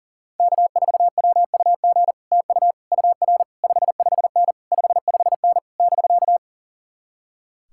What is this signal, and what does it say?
Morse code, heard as a keyed tone, about 49 wpm, 700 Hz. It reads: K4WUG TU UR 55N 55N <BK>
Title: Speed-Racing Common QSO Phrases - 40wpm